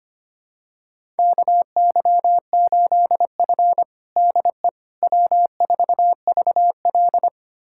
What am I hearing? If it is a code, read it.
Morse code, heard as a keyed tone, about 25 wpm, 700 Hz. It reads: KY8F DE W4VL
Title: USA Callsign DE Callsign - 4 Characters - 25wpm